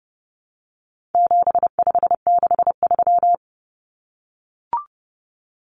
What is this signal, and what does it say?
Morse code, heard as a keyed tone, about 30 wpm, 700 Hz. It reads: 7563
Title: Numbers - 4-Digit 30wpm